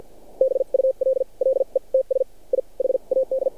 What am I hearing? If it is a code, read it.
Morse code, heard as a keyed tone, about 52 wpm, spaced as for 39 wpm, 530 Hz. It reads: BULLETS I HAD
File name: archived / CRSnet